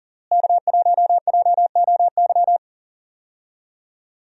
Morse code, heard as a keyed tone, about 40 wpm, 700 Hz. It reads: K1JOY
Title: USA Call Signs - 5 Characters - Repeated 2x - 40wpm